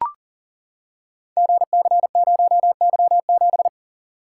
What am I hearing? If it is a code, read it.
Morse code, heard as a keyed tone, about 40 wpm, 700 Hz. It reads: GC0Y7
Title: Code Groups - 5 Letter:Numbers - 40wpm